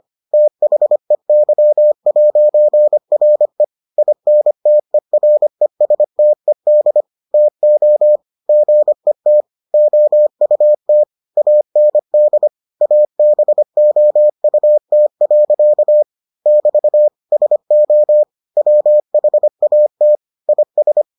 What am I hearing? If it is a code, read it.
Morse code, heard as a keyed tone, about 25 wpm, 600 Hz. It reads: THEY'RE INTERESTED TO GET OUT AND ABOUT. <BT> SO WHAT IS